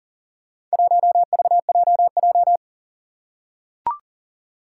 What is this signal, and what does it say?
Morse code, heard as a keyed tone, about 40 wpm, 700 Hz. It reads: 1VJJ
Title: Four Challenging Letter-Number Combinations 40wpm